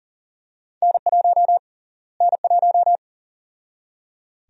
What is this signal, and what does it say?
Morse code, heard as a keyed tone, about 40 wpm, 700 Hz. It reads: N1 N1
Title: USA Call Sign Prefixes - Repeated 2x - 40wpm